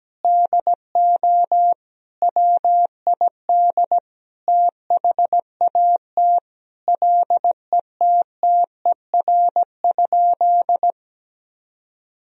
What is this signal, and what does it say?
Morse code, heard as a keyed tone, about 17 wpm, 700 Hz. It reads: DO WID THAT LETTER?
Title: Sentences from Top 300 Words and Abbreviations 17wpm